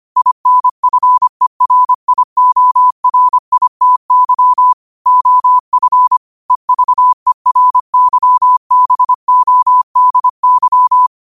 Morse code, heard as a keyed tone, about 25 wpm, 1000 Hz. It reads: INFERIORITY OF EVERYBODY